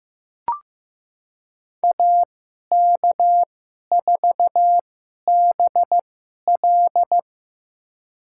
Morse code, heard as a keyed tone, about 15 wpm, spaced as for 11 wpm, 700 Hz. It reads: AK4BL
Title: USA Call Signs - 2x Character Spacing - 15wpm